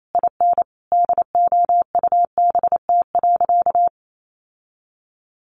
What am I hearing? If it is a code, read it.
Morse code, heard as a keyed tone, about 28 wpm, 700 Hz. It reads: IN DOUBT.
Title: Sentences from Top 1500 Words 28wpm